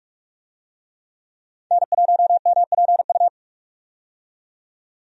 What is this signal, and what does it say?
Morse code, heard as a keyed tone, about 47 wpm, 700 Hz. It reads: N1MPU
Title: USA Call Signs - 5 Characters - 45wpm